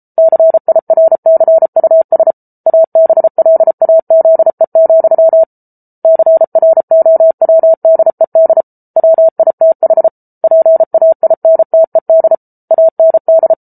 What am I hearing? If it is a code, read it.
Morse code, heard as a keyed tone, about 34 wpm, 650 Hz. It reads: CIRCUS ABLAZE, CROWDED WITH PAINTED AND